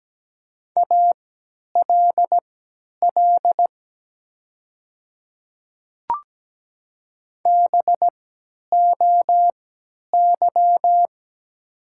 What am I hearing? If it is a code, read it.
Morse code, heard as a keyed tone, about 17 wpm, spaced as for 10 wpm, 700 Hz. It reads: ALL BOY
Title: 3x Character Spacing - Top 300 Words - 17wpm